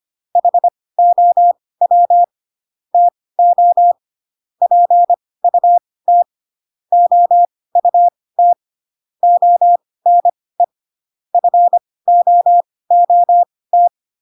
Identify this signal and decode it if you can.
Morse code, heard as a keyed tone, about 25 wpm, spaced as for 18 wpm, 700 Hz. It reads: HOW TO PUT OUT ONE FOOT
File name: Pinocchio ch3 12 25 ik1hge